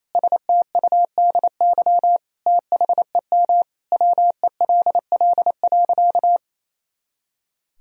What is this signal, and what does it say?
Morse code, heard as a keyed tone, about 28 wpm, 700 Hz. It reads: STUDY THEM WELL.
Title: Sentences from Top 300 Words and Abbreviations 28wpm